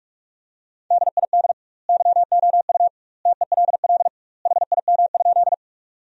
Morse code, heard as a keyed tone, about 43 wpm, 700 Hz. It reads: DID YOU TELL HIM?
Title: 2x Word-Spacing - Sentences from Top 200 Words - 45wpm